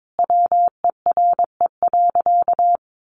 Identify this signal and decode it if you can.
Morse code, heard as a keyed tone, about 22 wpm, 700 Hz. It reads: WERE.
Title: Sentences from Top 1000 Words 22wpm